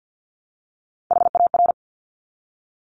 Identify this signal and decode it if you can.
Morse code, heard as a keyed tone, about 50 wpm, 700 Hz. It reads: HAR
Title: Top Three Letter Combinations in Words 50wpm